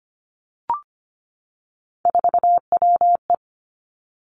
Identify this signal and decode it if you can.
Morse code, heard as a keyed tone, about 25 wpm, 700 Hz. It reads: 4WE
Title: Three Challenging Letter-Number Combinations 25wpm